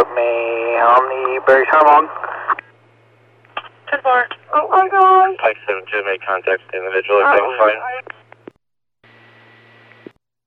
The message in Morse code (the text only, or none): none